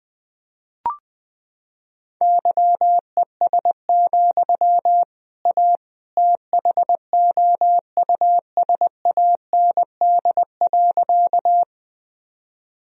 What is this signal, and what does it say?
Morse code, heard as a keyed tone, about 20 wpm, 700 Hz. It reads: YES, A THOUSAND.
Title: Sentences from Top 1000 Words - Review Part 3 - 20wpm